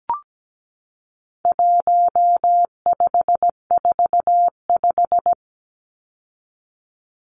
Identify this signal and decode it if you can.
Morse code, heard as a keyed tone, about 17 wpm, 700 Hz. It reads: 1545
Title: Numbers - 4-Digit 17wpm